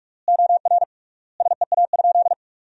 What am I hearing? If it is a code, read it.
Morse code, heard as a keyed tone, about 46 wpm, 700 Hz. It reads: OR SEA?
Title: 3x Word-Spacing - Sentences from Top 500 Words - 45wpm